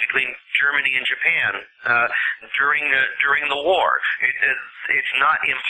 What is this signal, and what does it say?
Emergency Alert System (SAME) tones, heard as harsh buzzing bursts, with speech. It is heard from 1.2 to 5.1 s.